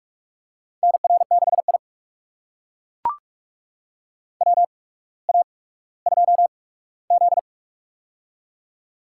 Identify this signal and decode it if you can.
Morse code, heard as a keyed tone, about 42 wpm, spaced as for 19 wpm, 700 Hz. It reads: NR6I WA2Z
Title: USA Call Signs - 4 Characters - 8x Character Spacing - 45wpm